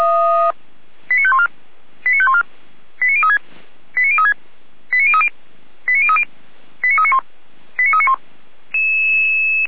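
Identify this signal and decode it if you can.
Five-tone selective call (ZVEI), heard as quick runs of five tones, tones of about 80 ms. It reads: 89514 89514 89036 89036 89030 89030 89391 89391
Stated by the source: SDR reception of ZVEI five-tone call 89140, from the feuerwehr-orlamuende.org fire brigade server.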